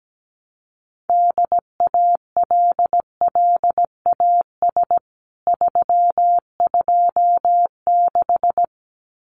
Morse code, heard as a keyed tone, about 17 wpm, 700 Hz. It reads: DALLAS 326
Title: CWT Exchange 17wpm